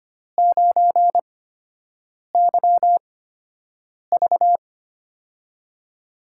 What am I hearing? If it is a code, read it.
Morse code, heard as a keyed tone, about 25 wpm, 700 Hz. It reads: 9 Y V
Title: USA Call Signs - 8x Character Spacing - 25wpm